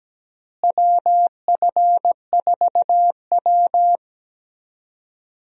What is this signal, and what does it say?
Morse code, heard as a keyed tone, about 17 wpm, 700 Hz. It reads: WF4W
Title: USA Call Signs - 4 Characters - Repeated 2x - 17wpm